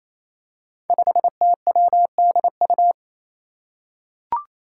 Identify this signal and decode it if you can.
Morse code, heard as a keyed tone, about 29 wpm, 700 Hz. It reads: 5TWDU
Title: Five Challenging Letter-Number Combinations 28wpm